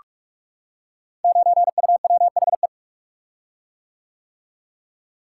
Morse code, heard as a keyed tone, about 44 wpm, 700 Hz. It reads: 9UWHE
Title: Code Groups - 5 Letter:Numbers - 45wpm